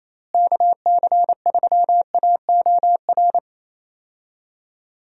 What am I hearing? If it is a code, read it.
Morse code, heard as a keyed tone, about 28 wpm, 700 Hz. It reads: KC3AOR
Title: USA Call Signs - 6 Characters - Repeated 2x - 28wpm